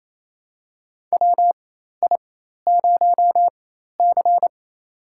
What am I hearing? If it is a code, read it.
Morse code, heard as a keyed tone, about 28 wpm, spaced as for 13 wpm, 700 Hz. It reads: WI0C
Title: USA Call Signs - 4 Characters - 4x Character Spacing - 28wpm